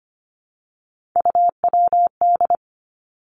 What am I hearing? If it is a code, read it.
Morse code, heard as a keyed tone, about 25 wpm, 700 Hz. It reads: UWD
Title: Three Challenging Letter-Number Combinations 25wpm